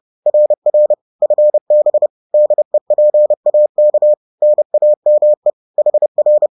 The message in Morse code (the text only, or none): RR FB DEPAK NAME HR